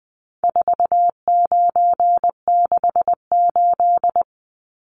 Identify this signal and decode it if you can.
Morse code, heard as a keyed tone, about 20 wpm, 700 Hz. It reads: 4968
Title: Numbers - 4-Digit 20wpm